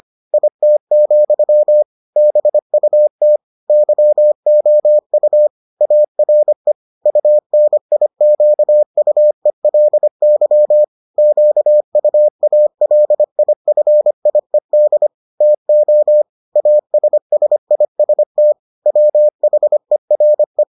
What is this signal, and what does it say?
Morse code, heard as a keyed tone, about 25 wpm, 600 Hz. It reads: IT, BUT YOU ARE UNIQUELY QUALIFIED TO ASSIST WHERE